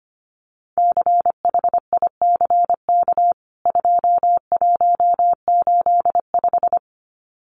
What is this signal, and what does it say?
Morse code, heard as a keyed tone, about 25 wpm, 700 Hz. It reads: CHICK 2185
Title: CWT Exchange 25wpm